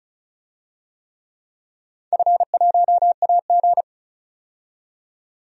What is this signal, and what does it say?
Morse code, heard as a keyed tone, about 36 wpm, 700 Hz. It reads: F1AG
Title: DXCC-100 Call Signs - 4 Characters - 35wpm